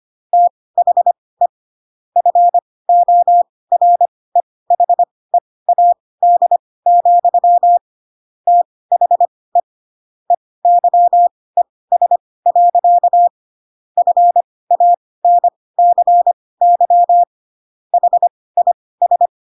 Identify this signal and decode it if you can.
Morse code, heard as a keyed tone, about 25 wpm, spaced as for 18 wpm, 700 Hz. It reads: THE FOREHEAD, THE EYES. FANCY HIS
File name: Pinocchio ch3 12 25 ik1hge